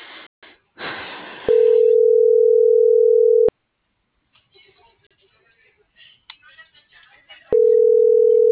Ringback tone (North America), 440 + 480 Hz.